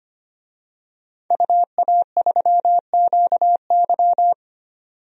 Morse code, heard as a keyed tone, about 25 wpm, 700 Hz. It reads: UA3QY
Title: DXCC-100 Call Signs - 5 Characters - 25wpm